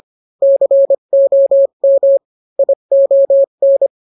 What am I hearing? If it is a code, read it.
Morse code, heard as a keyed tone, about 25 wpm, 550 Hz. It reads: COM ION